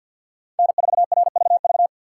Morse code, heard as a keyed tone, about 61 wpm, 700 Hz. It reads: N4RVV
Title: Speed Racing - USA 1x3 Call Signs 50wpm